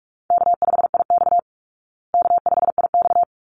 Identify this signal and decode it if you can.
Morse code, heard as a keyed tone, about 44 wpm, 700 Hz. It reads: K5IX K5IX
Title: Easy Call Signs Repeated 2x - 45wpm